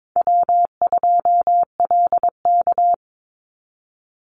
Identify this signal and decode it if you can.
Morse code, heard as a keyed tone, about 22 wpm, 700 Hz. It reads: W2LK
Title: Call Signs Repeated 3x - 22wpm